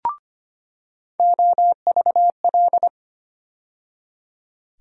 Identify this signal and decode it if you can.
Morse code, heard as a keyed tone, about 25 wpm, 700 Hz. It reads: OVL
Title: USA Call Sign Postfixes - 3 Letters 25wpm